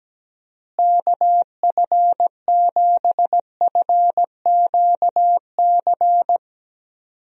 Morse code, heard as a keyed tone, about 17 wpm, 700 Hz. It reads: KF7FQC
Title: USA Call Signs - 6 Characters - Repeated 2x - 17wpm